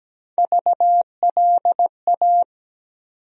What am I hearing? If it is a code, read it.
Morse code, heard as a keyed tone, about 17 wpm, 700 Hz. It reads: VLA
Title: USA Call Sign Postfixes - 3 Letters 17wpm